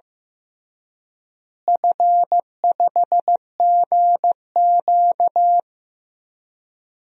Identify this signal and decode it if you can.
Morse code, heard as a keyed tone, about 15 wpm, 700 Hz. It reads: F5GQ
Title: DXCC-100 Call Signs - 4 Characters - Repeated 3x - 15wpm